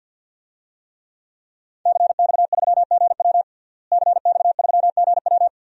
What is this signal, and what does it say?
Morse code, heard as a keyed tone, about 50 wpm, 700 Hz. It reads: CX3GW CX3GW
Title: DXCC-100 Call Signs - 5 Characters - Repeated 2x - 50wpm